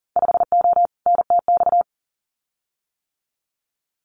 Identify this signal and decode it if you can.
Morse code, heard as a keyed tone, about 38 wpm, 700 Hz. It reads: 5O NTX
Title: Winter Field Day Exchanges Encore 40wpm